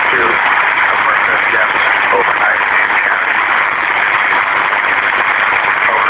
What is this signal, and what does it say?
Emergency Alert System (SAME) tones, heard as harsh buzzing bursts, with speech. It is heard from 1.2 to 5.2 s.